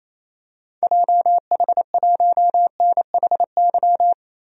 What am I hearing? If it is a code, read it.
Morse code, heard as a keyed tone, about 28 wpm, 700 Hz. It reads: JH1NHY
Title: DXCC-100 Call Signs - 6 Characters - 28wpm